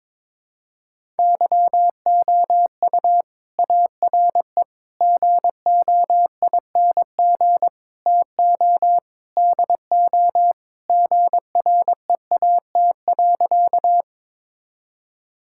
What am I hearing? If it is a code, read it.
Morse code, heard as a keyed tone, about 22 wpm, 700 Hz. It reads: YOU ARE GOING TO DO GREAT.
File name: practice / 22wpm / Sentences with 4-Word Phrases - 22wpm